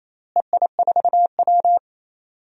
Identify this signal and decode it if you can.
Morse code, heard as a keyed tone, about 28 wpm, 700 Hz. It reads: EI4W
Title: DXCC-100 Call Signs - 4 Characters - Repeated 3x - 28wpm